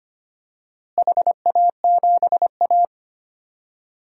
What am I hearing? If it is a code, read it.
Morse code, heard as a keyed tone, about 25 wpm, 700 Hz. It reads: HA7A
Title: DXCC-100 Call Signs - 4 Characters - Repeated 3x - 25wpm